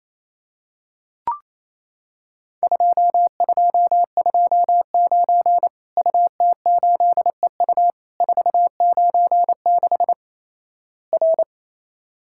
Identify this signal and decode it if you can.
Morse code, heard as a keyed tone, about 28 wpm, 700 Hz. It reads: 2229 UT8EU 496 R
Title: Worked All Europe DX Contest - QTC Exchange - 28wpm